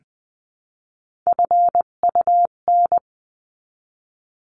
Morse code, heard as a keyed tone, about 20 wpm, 700 Hz. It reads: FUN
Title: 258 Lesson 42c - Words 16-20wpm